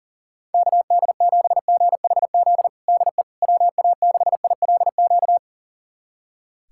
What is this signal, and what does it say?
Morse code, heard as a keyed tone, about 40 wpm, 700 Hz. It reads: KD7GHZ DE WA6ILQ
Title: USA Callsign DE Callsign - 6 Characters - 40wpm